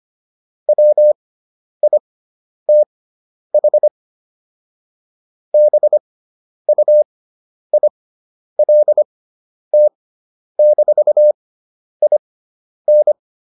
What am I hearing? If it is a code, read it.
Morse code, heard as a keyed tone, about 25 wpm, spaced as for 10 wpm, 600 Hz. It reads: WITH BUILT-IN